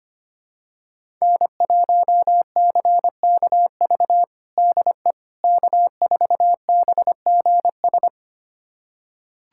Morse code, heard as a keyed tone, about 25 wpm, 700 Hz. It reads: N1CKV DE K4BGS
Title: USA Callsign DE Callsign - 5 Characters - 25wpm